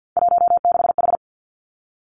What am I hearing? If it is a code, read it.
Morse code, heard as a keyed tone, about 50 wpm, 700 Hz. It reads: 16H
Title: Three Challenging Letter-Number Combinations 50wpm